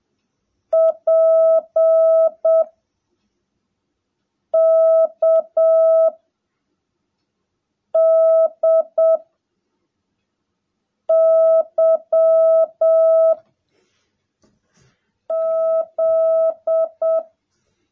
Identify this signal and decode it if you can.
Morse code, heard as a keyed tone, about 7 wpm, 660 Hz. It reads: P K D Y Z
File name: jh8cbh / 2021cw / cw18a